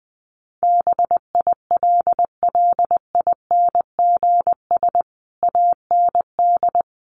Morse code, heard as a keyed tone, about 20 wpm, 700 Hz. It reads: BILLINGS AND
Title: A Sound Of Thunder 20wpm